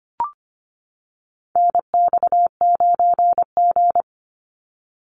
Morse code, heard as a keyed tone, about 25 wpm, 700 Hz. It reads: NX9G
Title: Call Signs - 25wpm